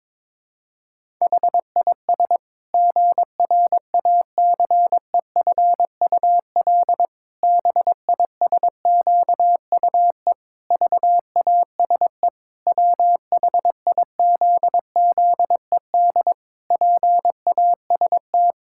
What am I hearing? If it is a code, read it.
Morse code, heard as a keyed tone, about 22 wpm, 700 Hz. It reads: HIS GRACEFUL BISQUE VASE WHIZZED PAST
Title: Pangram Challenge v2 - 22wpm